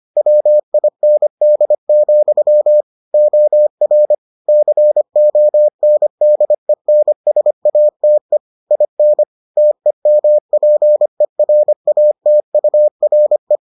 Morse code, heard as a keyed tone, about 25 wpm, 600 Hz. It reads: WIND, OR CONDENSATE IN TEMPERATURE